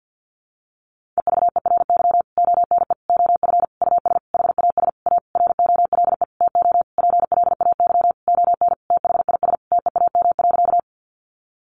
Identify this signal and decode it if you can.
Morse code, heard as a keyed tone, about 50 wpm, 700 Hz. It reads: EVERY ONE OF US HAS A ROLE TO PLAY ON THIS TEAM.
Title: Sentences from Top 1500 Words 50wpm